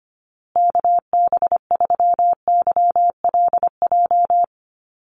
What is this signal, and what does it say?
Morse code, heard as a keyed tone, about 25 wpm, 700 Hz. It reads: KB3YLJ
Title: USA Call Signs Repeated 3x 25wpm